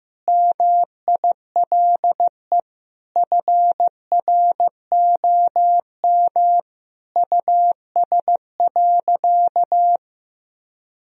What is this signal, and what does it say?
Morse code, heard as a keyed tone, about 15 wpm, 700 Hz. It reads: MILE FROM US.
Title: Sentences from Top 1000 Words - Review Part 1 - 15wpm